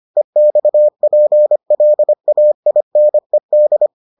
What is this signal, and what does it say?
Morse code, heard as a keyed tone, about 25 wpm, 600 Hz. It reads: EXPLAINED